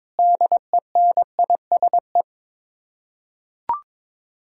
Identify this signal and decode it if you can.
Morse code, heard as a keyed tone, about 22 wpm, 700 Hz. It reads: DENISE
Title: Practice QSO - Exchange 1 - Beginning - 22wpm